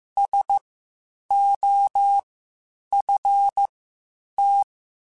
Morse code, heard as a keyed tone, about 15 wpm, spaced as for 8 wpm, 800 Hz. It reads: SOFT
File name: Spaced / Chuckles Spaced-mp3s / SoftWords sp @13